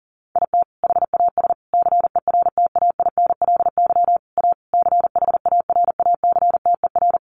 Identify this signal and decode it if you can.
Morse code, heard as a keyed tone, about 40 wpm, 700 Hz. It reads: IT HAS CERTAINLY A CHARACTER
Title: Sentences from Top 1200 Words 40wpm